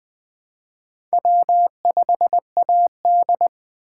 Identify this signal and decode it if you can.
Morse code, heard as a keyed tone, about 20 wpm, 700 Hz. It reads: W5AD